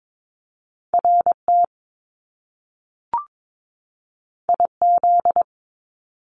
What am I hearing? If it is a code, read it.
Morse code, heard as a keyed tone, about 22 wpm, 700 Hz. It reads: RT IZ
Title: Top Two Letter Combinations in Words 22wpm